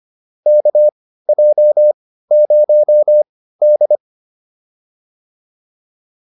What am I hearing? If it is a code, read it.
Morse code, heard as a keyed tone, about 25 wpm, spaced as for 15 wpm, 600 Hz. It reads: KJ0D